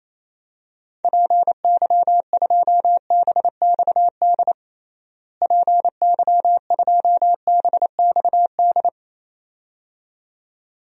Morse code, heard as a keyed tone, about 28 wpm, 700 Hz. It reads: PY2BXD PY2BXD
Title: DXCC-100 Call Signs - 6 Characters - Repeated 2x - 28wpm